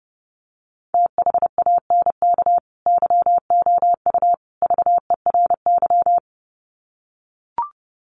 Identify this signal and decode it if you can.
Morse code, heard as a keyed tone, about 30 wpm, 700 Hz. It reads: THANK YOU VERY
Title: Sets of 3 Words 30wpm